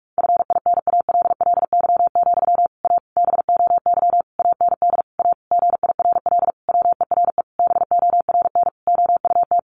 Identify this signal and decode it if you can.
Morse code, heard as a keyed tone, about 44 wpm, 700 Hz. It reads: FINALLY, A BOY AND A GIRL WERE BORN OUT
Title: Sentences from Top 1200 Words 45wpm